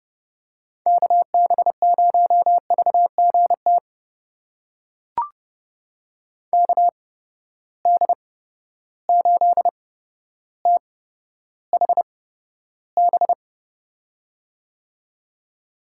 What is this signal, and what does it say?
Morse code, heard as a keyed tone, about 31 wpm, spaced as for 13 wpm, 700 Hz. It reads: KB0VGT KD8THB
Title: USA Call Signs - 6 Characters - 8x Character Spacing - 30wpm